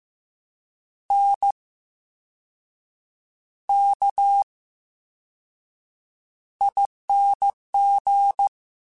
Morse code, heard as a keyed tone, about 15 wpm, spaced as for 6 wpm, 800 Hz. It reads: NKING
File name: Special INGs / Special INGs-mp3s / Slow ing @13